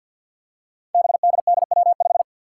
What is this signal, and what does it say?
Morse code, heard as a keyed tone, about 46 wpm, 700 Hz. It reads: BDDW5
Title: Five Challenging Letter-Number Combinations 50wpm